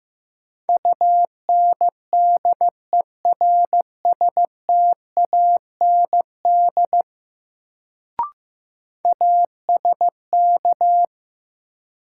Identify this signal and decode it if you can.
Morse code, heard as a keyed tone, about 15 wpm, 700 Hz. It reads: UNDERSTAND ASK
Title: Sets of 4 Words - New Word Drill - 15wpm